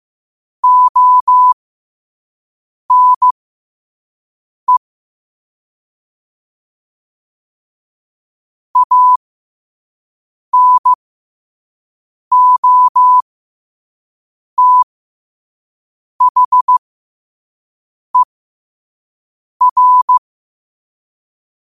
Morse code, heard as a keyed tone, about 15 wpm, spaced as for 5 wpm, 1000 Hz. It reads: ONE ANOTHER
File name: qotdfiles / 5wpm